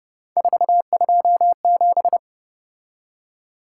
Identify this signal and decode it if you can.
Morse code, heard as a keyed tone, about 29 wpm, 700 Hz. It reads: 427